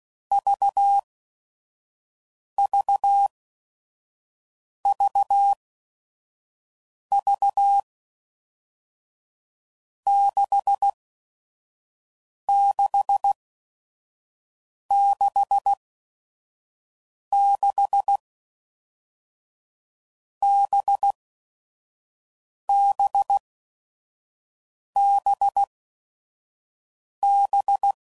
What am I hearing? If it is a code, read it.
Morse code, heard as a keyed tone, about 16 wpm, spaced as for 5 wpm, 800 Hz. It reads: VVVV 6666 BBBB
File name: Groups / Basic Letter Groups / VB46